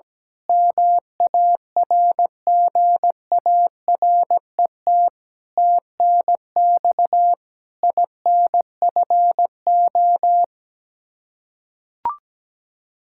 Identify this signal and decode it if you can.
Morse code, heard as a keyed tone, about 17 wpm, 700 Hz. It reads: MARGARET TNX INFO
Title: Practice QSO - Exchange 3 - More Info - 17wpm